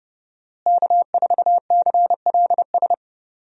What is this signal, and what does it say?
Morse code, heard as a keyed tone, about 30 wpm, 700 Hz. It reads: K4CLS
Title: USA Call Signs - 5 Characters - Repeated 2x - 30wpm